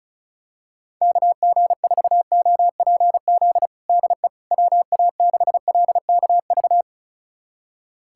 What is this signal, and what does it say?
Morse code, heard as a keyed tone, about 36 wpm, 700 Hz. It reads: KG4OPZ DE WA6LKV
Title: USA Callsign DE Callsign - 6 Characters - 35wpm